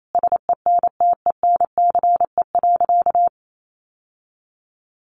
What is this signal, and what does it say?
Morse code, heard as a keyed tone, about 28 wpm, 700 Hz. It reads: SENTENCE.
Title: Sentences from Top 1200 Words 28wpm